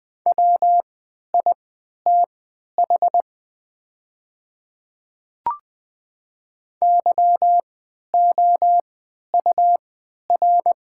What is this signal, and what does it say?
Morse code, heard as a keyed tone, about 20 wpm, spaced as for 11 wpm, 700 Hz. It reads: WITH YOUR
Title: 3x Character Spacing - Top 100 Words - 20wpm